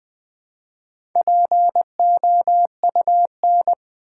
Morse code, heard as a keyed tone, about 20 wpm, 700 Hz. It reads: POUN
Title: Build A Word - 1 Letter At A Time - 3 to 5-Letter Words - 20wpm